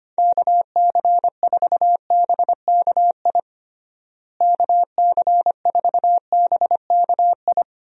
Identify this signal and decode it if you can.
Morse code, heard as a keyed tone, about 25 wpm, 700 Hz. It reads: KC4BKI KC4BKI
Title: USA Call Signs - 6 Characters - Repeated 3x - 25wpm